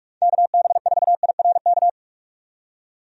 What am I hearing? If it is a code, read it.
Morse code, heard as a keyed tone, about 46 wpm, 700 Hz. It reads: KB4IRK
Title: USA Call Signs - 6 Characters - 45wpm